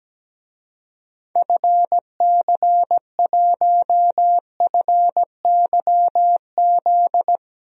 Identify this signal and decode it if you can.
Morse code, heard as a keyed tone, about 17 wpm, 700 Hz. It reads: FC1FYZ
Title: DXCC-100 Call Signs - 6 Characters - 17wpm